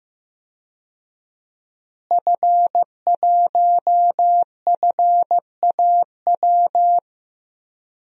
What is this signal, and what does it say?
Morse code, heard as a keyed tone, about 15 wpm, 700 Hz. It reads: F1FAW
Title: DXCC-100 Call Signs - 5 Characters - Repeated 3x - 15wpm